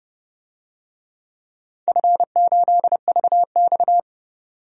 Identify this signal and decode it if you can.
Morse code, heard as a keyed tone, about 30 wpm, 700 Hz. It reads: F8VX
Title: DXCC-100 Call Signs - 4 Characters - 30wpm